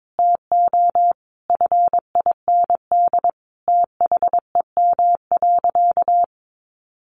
Morse code, heard as a keyed tone, about 22 wpm, 700 Hz. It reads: TO FIND THEM.
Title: Sentences from Top 1500 Words 22wpm